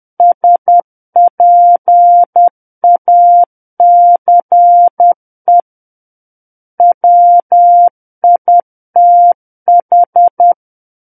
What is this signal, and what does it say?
Morse code, heard as a keyed tone, about 10 wpm, 700 Hz. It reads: SPACE WITH
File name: Book One / 01-Chapter 1 - 10 WPM